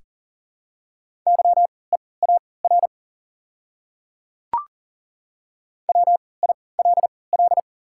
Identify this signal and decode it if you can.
Morse code, heard as a keyed tone, about 40 wpm, spaced as for 23 wpm, 700 Hz. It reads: YEAR WILL
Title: 3x Character Spacing - Top 200 Words - 40wpm